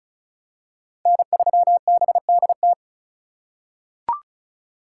Morse code, heard as a keyed tone, about 34 wpm, 700 Hz. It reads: N3BDT
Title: USA Call Signs - 5 Characters - 2x Character Spacing - 35wpm